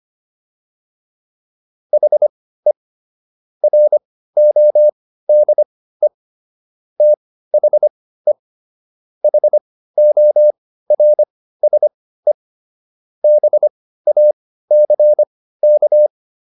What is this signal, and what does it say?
Morse code, heard as a keyed tone, about 25 wpm, spaced as for 15 wpm, 600 Hz. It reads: HE RODE THE HORSE BACK